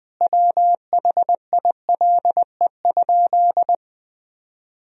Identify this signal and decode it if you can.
Morse code, heard as a keyed tone, about 20 wpm, 700 Hz. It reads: WHILE?
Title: Sentences from Top 300 Words and Abbreviations 20wpm